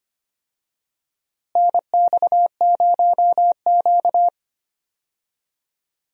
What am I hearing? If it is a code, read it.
Morse code, heard as a keyed tone, about 25 wpm, 700 Hz. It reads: NX0Q
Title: USA Call Signs - 4 Characters - Repeated 3x - 25wpm